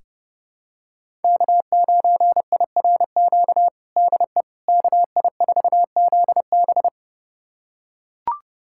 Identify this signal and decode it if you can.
Morse code, heard as a keyed tone, about 30 wpm, 700 Hz. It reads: K9IRQ DE KI4ZB
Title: USA Callsign DE Callsign - 5 Characters - 30wpm